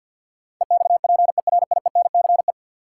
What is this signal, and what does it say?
Morse code, heard as a keyed tone, about 46 wpm, 700 Hz. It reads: EXPERIENCE
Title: Sentences from Top 600 Words - New Word Drill - 50wpm